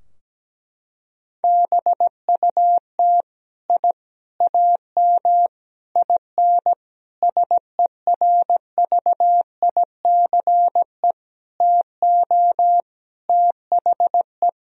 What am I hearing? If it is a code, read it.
Morse code, heard as a keyed tone, about 17 wpm, 700 Hz. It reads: BUT I AM IN SERVICE TO THE
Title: Sentences from Top 1000 Words - Review Part 4 - 17wpm